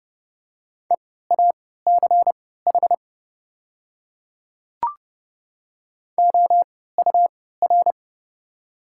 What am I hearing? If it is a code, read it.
Morse code, heard as a keyed tone, about 31 wpm, spaced as for 17 wpm, 700 Hz. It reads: EACH OUR
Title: 3x Character Spacing - Top 200 Words - 30wpm